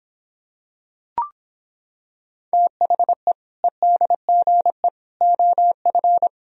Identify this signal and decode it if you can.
Morse code, heard as a keyed tone, about 26 wpm, 700 Hz. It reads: THE EDGE OF